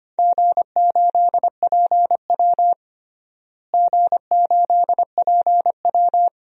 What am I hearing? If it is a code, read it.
Morse code, heard as a keyed tone, about 25 wpm, 700 Hz. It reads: G8PW G8PW
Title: DXCC-100 Call Signs - 4 Characters - Repeated 2x - 25wpm